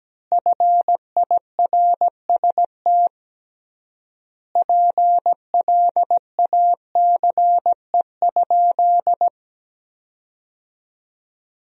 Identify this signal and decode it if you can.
Morse code, heard as a keyed tone, about 17 wpm, 700 Hz. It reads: FIRST PLACE?
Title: 3x Word-Spacing - Sentences from Top 200 Words - 17wpm